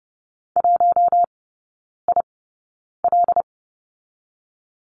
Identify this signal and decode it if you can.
Morse code, heard as a keyed tone, about 30 wpm, 700 Hz. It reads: 1 I L
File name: Lesson 19e - Letter w - Three Characters 30wpm